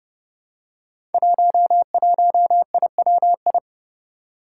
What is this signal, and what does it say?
Morse code, heard as a keyed tone, about 30 wpm, 700 Hz. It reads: 11IWI
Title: Five Challenging Letter-Number Combinations 30wpm